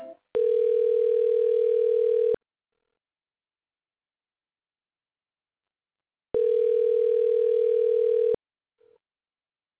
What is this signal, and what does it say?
Ringback tone (North America), 440 + 480 Hz.